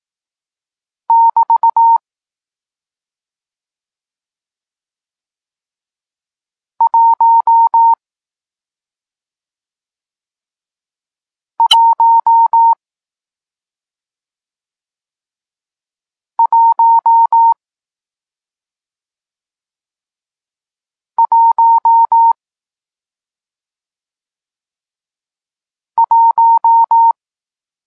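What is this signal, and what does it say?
Morse code, heard as a keyed tone, about 18 wpm, 920 Hz. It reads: <BT> 1 1 1 1 1
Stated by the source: Morse code lesson teaching digits 1, 4 and 6, from SP9MOA, Niepołomicki Klub Krótkofalowców (Poland) amateur radio club.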